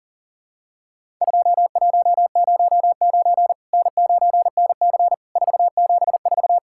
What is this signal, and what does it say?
Morse code, heard as a keyed tone, about 37 wpm, 700 Hz. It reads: 2109 N9NC 474
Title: Worked All Europe DX Contest - QTC Exchange - 40wpm